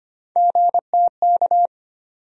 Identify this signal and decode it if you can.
Morse code, heard as a keyed tone, about 25 wpm, 700 Hz. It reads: GTK
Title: USA Call Sign Postfixes - 3 Letters 25wpm